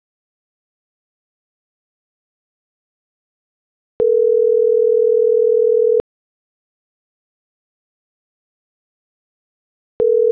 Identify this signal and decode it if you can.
Ringback tone (North America), 440 + 480 Hz.